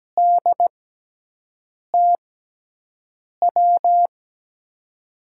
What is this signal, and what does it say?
Morse code, heard as a keyed tone, about 17 wpm, spaced as for 6 wpm, 700 Hz. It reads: DTW
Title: USA Call Signs - 6 Characters - 6x Character Spacing - 17wpm